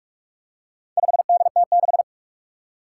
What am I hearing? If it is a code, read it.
Morse code, heard as a keyed tone, about 46 wpm, 700 Hz. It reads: 5DT6
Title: Four Challenging Letter-Number Combinations 45wpm